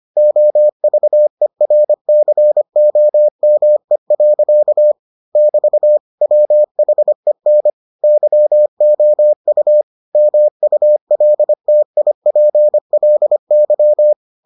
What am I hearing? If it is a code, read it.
Morse code, heard as a keyed tone, about 25 wpm, 600 Hz. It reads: OVERCOME. <BT> WHEN YOU MULTIPLY